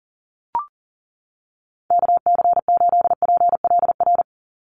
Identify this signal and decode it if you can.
Morse code, heard as a keyed tone, about 40 wpm, 700 Hz. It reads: KC8PLR
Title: USA Call Signs 40wpm